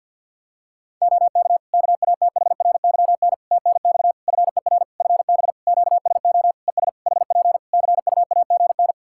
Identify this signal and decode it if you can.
Morse code, heard as a keyed tone, about 52 wpm, 700 Hz. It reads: OK KATHRYN TNX FER FB QSO ES HP CUAGN
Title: K1USN SST QSO Practice 50wpm